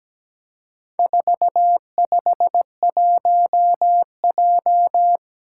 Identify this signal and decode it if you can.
Morse code, heard as a keyed tone, about 17 wpm, 700 Hz. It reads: 451J
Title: Four Challenging Letter-Number Combinations 17wpm